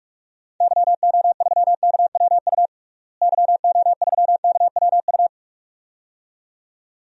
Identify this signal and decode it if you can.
Morse code, heard as a keyed tone, about 46 wpm, 700 Hz. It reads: YO3KWU YO3KWU
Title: DXCC-100 Call Signs - 6 Characters - Repeated 2x - 45wpm